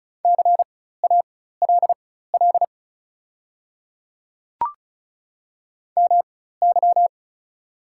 Morse code, heard as a keyed tone, about 36 wpm, spaced as for 16 wpm, 690 Hz. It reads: CALL MY
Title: 4x Character Spacing - Top 100 Words - 35wpm